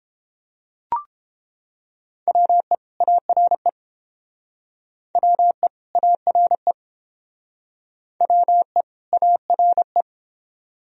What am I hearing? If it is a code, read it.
Morse code, heard as a keyed tone, about 31 wpm, 700 Hz. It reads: WE ARE WE ARE WE ARE